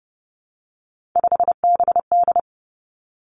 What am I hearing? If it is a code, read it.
Morse code, heard as a keyed tone, about 30 wpm, 700 Hz. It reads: 5BD
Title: Three Challenging Letter-Number Combinations 30wpm